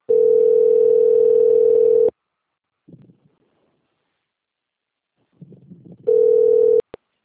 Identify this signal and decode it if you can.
Ringback tone (North America), 440 + 480 Hz.
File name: out- - - - - .1106